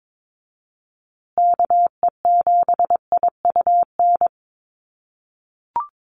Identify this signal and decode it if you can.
Morse code, heard as a keyed tone, about 22 wpm, 700 Hz. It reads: KE7IUN
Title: USA Call Signs Repeated 3x 22wpm